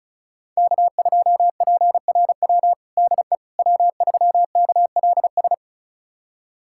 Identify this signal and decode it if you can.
Morse code, heard as a keyed tone, about 36 wpm, 700 Hz. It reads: K2PRW DE W3KLS
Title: USA Callsign DE Callsign - 5 Characters - 35wpm